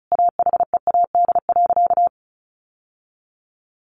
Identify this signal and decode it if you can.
Morse code, heard as a keyed tone, about 34 wpm, 700 Hz. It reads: AHEAD.
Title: Sentences from Top 1100 Words 35wpm